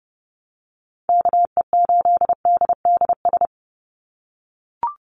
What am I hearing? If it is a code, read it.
Morse code, heard as a keyed tone, about 30 wpm, 700 Hz. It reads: KE8DDS
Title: USA Call Signs 30wpm